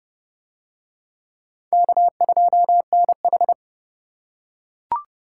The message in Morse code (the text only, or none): K2NH